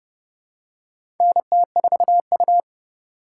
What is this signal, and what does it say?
Morse code, heard as a keyed tone, about 30 wpm, 700 Hz. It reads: NT4U
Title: USA Call Signs - 4 Characters - 4x Character Spacing - 30wpm